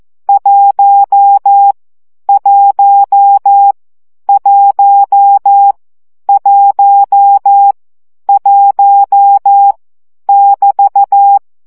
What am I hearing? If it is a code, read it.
Morse code, heard as a keyed tone, about 14 wpm, 800 Hz. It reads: 1 1 1 1 1 <BT>
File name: CW gyakorlas 1 12 baud